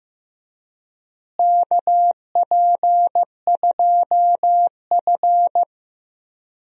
Morse code, heard as a keyed tone, about 16 wpm, 700 Hz. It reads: KP2F
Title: DXCC-100 Call Signs - 4 Characters - Repeated 3x - 15wpm